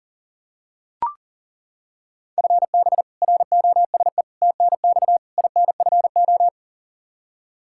Morse code, heard as a keyed tone, about 40 wpm, 700 Hz. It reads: FB ROSE TNX INFO
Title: Practice QSO - Exchange 3 - More Info - 40wpm